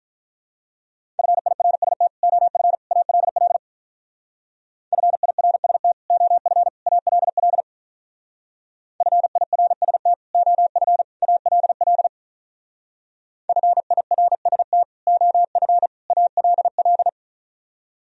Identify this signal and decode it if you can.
Morse code, heard as a keyed tone, about 44 wpm, 700 Hz. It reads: FIRST OF ALL FIRST OF ALL FIRST OF ALL FIRST OF ALL